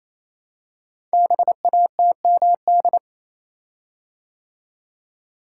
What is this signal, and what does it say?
Morse code, heard as a keyed tone, about 27 wpm, 700 Hz. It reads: BATMD
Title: Code Groups - 5 Letters - 28wpm